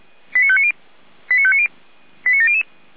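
Five-tone selective call (ZVEI), heard as quick runs of five tones, tones of about 70 ms. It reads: 89590 89590 89700
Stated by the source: ZVEI five-tone selective call 89270 received with an SDR.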